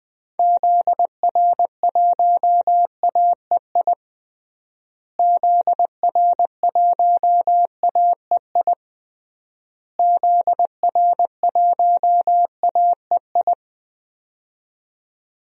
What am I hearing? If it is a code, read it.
Morse code, heard as a keyed tone, about 20 wpm, 700 Hz. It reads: ZR1AEI ZR1AEI ZR1AEI
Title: DXCC-100 Call Signs - 6 Characters - Repeated 3x - 20wpm